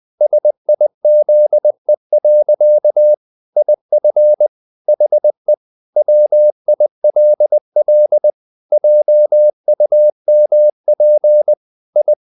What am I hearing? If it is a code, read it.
Morse code, heard as a keyed tone, about 20 wpm, 600 Hz. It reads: SIZE. IF HE WILL JUMP I
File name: the old man and the sea / the old man and the sea part 1 20wpm